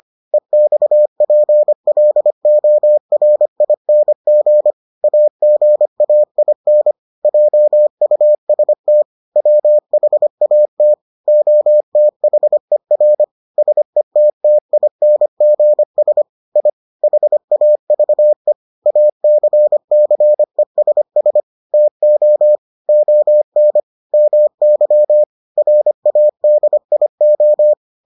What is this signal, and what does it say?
Morse code, heard as a keyed tone, about 25 wpm, 600 Hz. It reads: EXPLORING AGAIN JUST WHAT OTHER SETTINGS I HAVE ACCESS TO ON MY RADIO